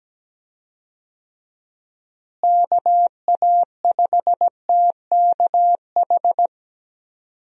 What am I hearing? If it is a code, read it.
Morse code, heard as a keyed tone, about 17 wpm, 700 Hz. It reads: KA5TKH
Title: USA Call Signs - 6 Characters - Repeated 2x - 17wpm